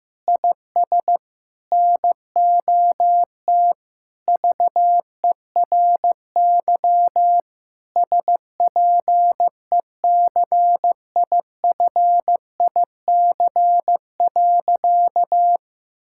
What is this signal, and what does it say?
Morse code, heard as a keyed tone, about 15 wpm, 700 Hz. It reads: IS NOT VERY SPECIFIC.
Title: Sentences from Top 1000 Words - Review Part 6 - 15wpm